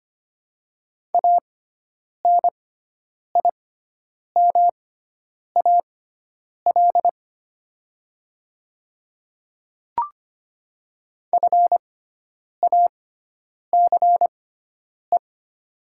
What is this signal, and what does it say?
Morse code, heard as a keyed tone, about 25 wpm, spaced as for 9 wpm, 700 Hz. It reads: ANIMAL FACE
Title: 6x Character Spacing - Top 500 Words - 25wpm